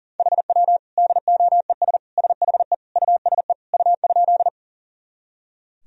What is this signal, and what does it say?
Morse code, heard as a keyed tone, about 40 wpm, 700 Hz. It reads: HW DOES SHE USE U?
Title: Sentences from Top 200 Words and Abbreviations 40wpm